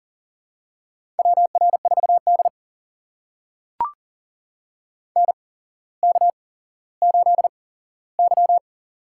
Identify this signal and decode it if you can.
Morse code, heard as a keyed tone, about 40 wpm, spaced as for 17 wpm, 700 Hz. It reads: WR4D NK8Y
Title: USA Call Signs - 4 Characters - 8x Character Spacing - 40wpm